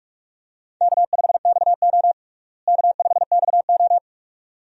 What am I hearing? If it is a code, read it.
Morse code, heard as a keyed tone, about 42 wpm, 700 Hz. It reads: K5XO K5XO
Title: USA Call Signs - 4 Characters - Repeated 2x - 45wpm